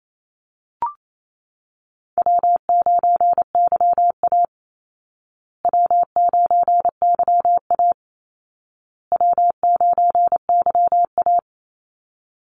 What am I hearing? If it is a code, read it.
Morse code, heard as a keyed tone, about 28 wpm, 700 Hz. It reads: W9YA W9YA W9YA
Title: Call Signs Repeated 2x - 28wpm